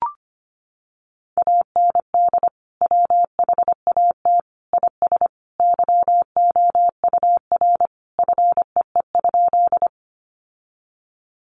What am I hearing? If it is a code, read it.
Morse code, heard as a keyed tone, about 25 wpm, 700 Hz. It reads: AND WHAT IS YOUR FEE?
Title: Sentences from Top 1500 Words 25wpm